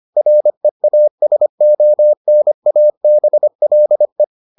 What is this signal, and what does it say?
Morse code, heard as a keyed tone, about 25 wpm, 600 Hz. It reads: REASONABLE